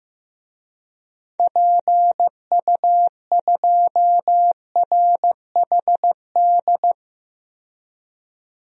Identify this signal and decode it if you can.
Morse code, heard as a keyed tone, about 16 wpm, 700 Hz. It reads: PU2RHD
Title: DXCC-100 Call Signs - 6 Characters - 15wpm